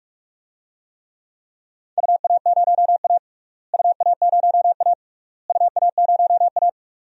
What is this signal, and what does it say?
Morse code, heard as a keyed tone, about 46 wpm, 700 Hz. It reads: UA0A UA0A UA0A
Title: DXCC-100 Call Signs - 4 Characters - Repeated 3x - 45wpm